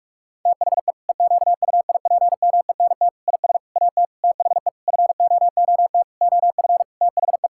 Morse code, heard as a keyed tone, about 41 wpm, 700 Hz. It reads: THE EQUIPMENT IS AT THE FOOT OF THE
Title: Sentences from Top 1000 Words - Review Part 5 - 45wpm